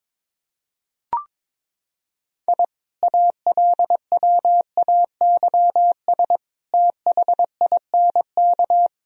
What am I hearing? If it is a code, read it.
Morse code, heard as a keyed tone, about 21 wpm, 700 Hz. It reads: I ALWAYS THINK
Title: Sentences from Top 1000 Words - Review Part 3 - 22wpm